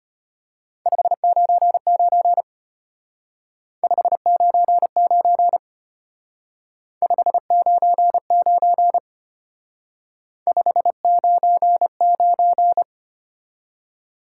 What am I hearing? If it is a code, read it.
Morse code, heard as a keyed tone, about 31 wpm, 700 Hz. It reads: 599 599 599 599
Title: Speed-Racing Top 90 QSO Elements 25wpm